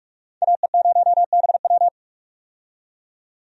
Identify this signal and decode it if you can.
Morse code, heard as a keyed tone, about 46 wpm, 700 Hz. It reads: AE0BW
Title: USA Call Signs - 5 Characters - 45wpm